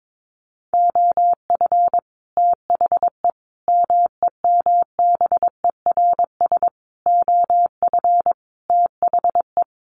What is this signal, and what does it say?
Morse code, heard as a keyed tone, about 22 wpm, 700 Hz. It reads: OF THE MEMBERS OF THE
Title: Sets of 5 Words Encore2 22wpm